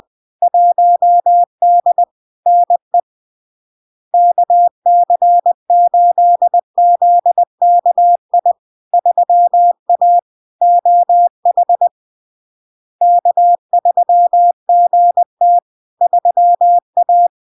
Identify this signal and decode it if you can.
Morse code, heard as a keyed tone, about 20 wpm, 700 Hz. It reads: 1D NE KC8ZKI 3A OH K3GT 3A
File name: LARK-FD-2022-Log-in-CW-20wpm-700-Hz